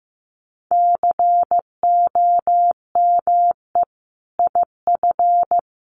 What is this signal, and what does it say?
Morse code, heard as a keyed tone, about 14 wpm, 700 Hz. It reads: COME IF
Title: Sentences from Top 1000 Words 15wpm